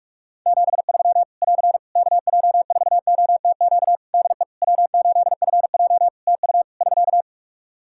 Morse code, heard as a keyed tone, about 45 wpm, 700 Hz. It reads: 73 <AR> KJ4OTQ DE W8FJ TU <SK>
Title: Practice QSO - Exchange 4 - Ending - 45wpm